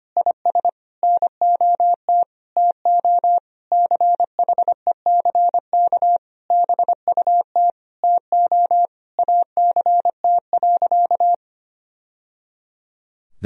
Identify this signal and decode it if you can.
Morse code, heard as a keyed tone, about 25 wpm, 700 Hz. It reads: IS NOT TO CHECK BUT TO ACT.
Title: Sentences from Top 1000 Words - Review Part 3 - 25wpm